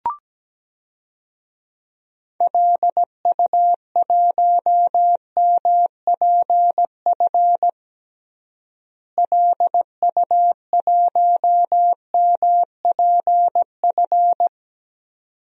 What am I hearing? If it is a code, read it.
Morse code, heard as a keyed tone, about 17 wpm, 700 Hz. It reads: LU1MPF LU1MPF
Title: DXCC-100 Call Signs - 6 Characters - Repeated 2x - 17wpm